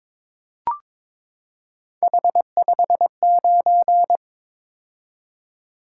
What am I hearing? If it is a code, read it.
Morse code, heard as a keyed tone, about 22 wpm, 700 Hz. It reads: H59
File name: practice / 22wpm / Three Letter-Number Combinations - 22wpm